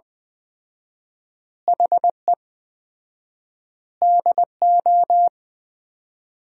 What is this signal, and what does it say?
Morse code, heard as a keyed tone, about 20 wpm, 700 Hz. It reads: HE DO
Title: 4x Word-Spacing - Sentences from Top 100 Words - 20wpm